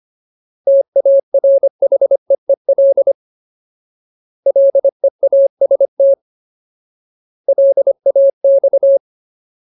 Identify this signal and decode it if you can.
Morse code, heard as a keyed tone, about 25 wpm, 550 Hz. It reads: TARHEEL LEAST LAX
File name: Ultimate Word List / 25WPM Ultimate